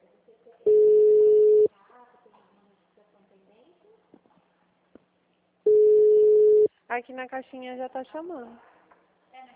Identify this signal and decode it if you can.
Ringback tone (Europe, 425 Hz), with speech.